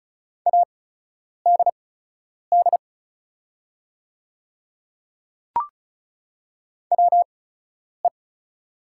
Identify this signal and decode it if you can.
Morse code, heard as a keyed tone, about 34 wpm, spaced as for 10 wpm, 690 Hz. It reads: ADD WE
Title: 8x Character Spacing - Top 200 Words - 35wpm